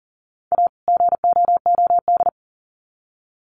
Morse code, heard as a keyed tone, about 40 wpm, 700 Hz. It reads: A GOOD
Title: Sets of 2 Words 40wpm